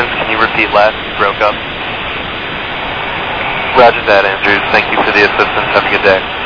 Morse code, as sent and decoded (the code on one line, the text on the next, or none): none